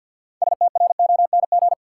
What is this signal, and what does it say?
Morse code, heard as a keyed tone, about 53 wpm, 700 Hz. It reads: STRONG
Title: Six Letter Words - 50wpm